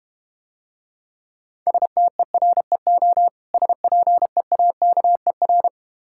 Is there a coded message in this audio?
Yes: Morse code from 1.7 to 5.7 s, heard as a keyed tone, about 32 wpm, 700 Hz. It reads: STEREO SPEAKER